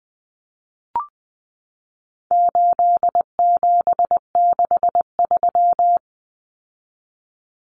Morse code, heard as a keyed tone, about 20 wpm, 700 Hz. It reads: 8763
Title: Common Cut Numbers - 4-Digit 20wpm